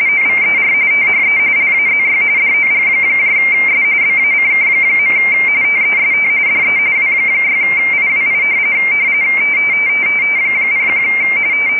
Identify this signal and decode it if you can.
RTTY, about 45.45 baud. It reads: OVER THE LAZY DOGS BACK. TEST DE W6IWI 01
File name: rtty / websdr recording 14086.0kHz